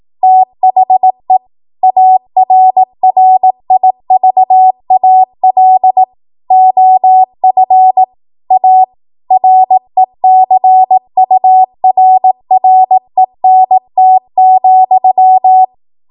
Morse code, heard as a keyed tone, about 18 wpm, 750 Hz. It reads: THE ARRIVAL OF A RECURRENT,
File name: Morse / 18WPM